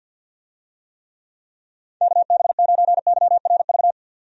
Morse code, heard as a keyed tone, about 51 wpm, 700 Hz. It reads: KB9YRV
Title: USA Call Signs - 6 Characters - 50wpm